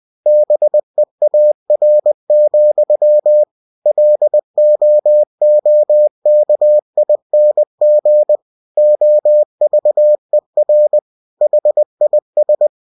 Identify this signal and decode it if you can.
Morse code, heard as a keyed tone, about 20 wpm, 600 Hz. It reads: BEAR, LOOKING OVER HIS
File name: cw-books / winnie-the-pooh / winnie-20wpm-page-0098